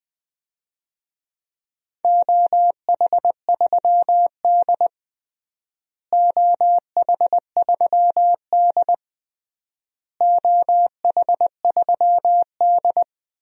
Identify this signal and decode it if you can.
Morse code, heard as a keyed tone, about 20 wpm, 700 Hz. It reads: OH3D OH3D OH3D
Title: DXCC-100 Call Signs - 4 Characters - Repeated 3x - 20wpm